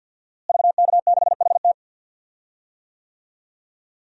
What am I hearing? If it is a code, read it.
Morse code, heard as a keyed tone, about 52 wpm, 700 Hz. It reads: VK6HT
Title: DXCC-100 Call Signs - 5 Characters - Repeated 3x - 50wpm